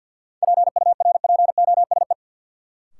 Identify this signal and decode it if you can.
Morse code, heard as a keyed tone, about 53 wpm, 700 Hz. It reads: PURPOSE
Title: Sets of 5 Words - New Word Drill - 50wpm